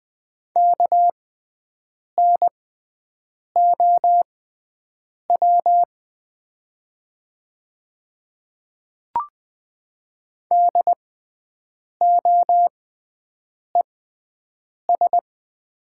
Morse code, heard as a keyed tone, about 20 wpm, spaced as for 7 wpm, 700 Hz. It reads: KNOW DOES
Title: 6x Character Spacing - Top 500 Words - 20wpm